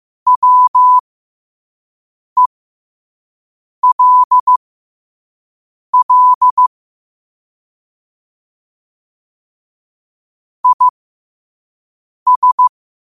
Morse code, heard as a keyed tone, about 15 wpm, spaced as for 5 wpm, 1000 Hz. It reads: WELL IS
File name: 5wpm